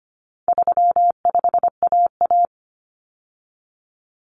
Morse code, heard as a keyed tone, about 25 wpm, 700 Hz. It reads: 35AA
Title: Common Cut Numbers - 4-Digit 25wpm